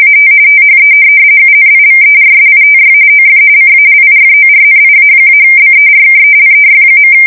RTTY, about 45.45 baud. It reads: AT THE UNIVERSITY OF CALIFORNIA-BERKELEY.
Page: 170 Shift 2125/2295